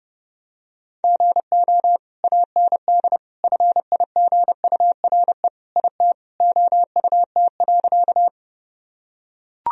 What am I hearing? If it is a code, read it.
Morse code, heard as a keyed tone, about 32 wpm, 700 Hz. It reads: GO AND FIGURE IT OUT.
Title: Sentences from Top 1000 Words - Review Part 2 - 30wpm